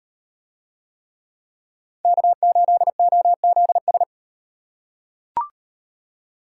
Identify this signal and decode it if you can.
Morse code, heard as a keyed tone, about 36 wpm, 700 Hz. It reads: K8OZS